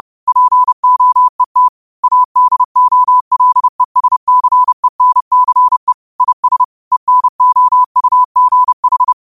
Morse code, heard as a keyed tone, about 30 wpm, 1000 Hz. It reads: POET ADOLESCENCE IS ENOUGH
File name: qotdfiles / 30wpm